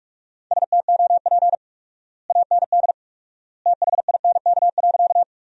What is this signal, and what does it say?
Morse code, heard as a keyed tone, about 47 wpm, 700 Hz. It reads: STOP AND THINK.
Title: 4x Word-Spacing - Sentences from Top 300 Words - 45wpm